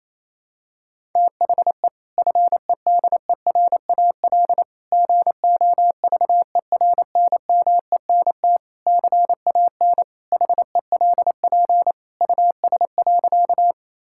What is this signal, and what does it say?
Morse code, heard as a keyed tone, about 28 wpm, 700 Hz. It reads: THE FEDERAL GOVERNMENT CAN HELP US.
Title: Sentences from Top 1000 Words - Review Part 5 - 28wpm